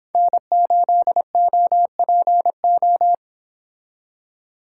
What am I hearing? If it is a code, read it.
Morse code, heard as a keyed tone, about 26 wpm, 700 Hz. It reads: N8OPO